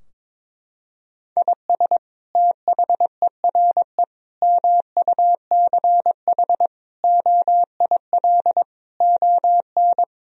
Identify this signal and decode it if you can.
Morse code, heard as a keyed tone, about 22 wpm, 700 Hz. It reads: IS THERE MUCH OIL ON
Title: Sentences from Top 1000 Words - Review Part 4 - 22wpm